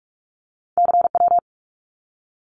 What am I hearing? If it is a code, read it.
Morse code, heard as a keyed tone, about 44 wpm, 700 Hz. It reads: CW
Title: Abbreviations 45wpm